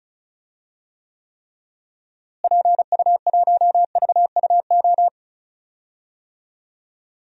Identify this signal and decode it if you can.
Morse code, heard as a keyed tone, about 34 wpm, 700 Hz. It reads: PU1VUO
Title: DXCC-100 Call Signs - 6 Characters - Repeated 3x - 35wpm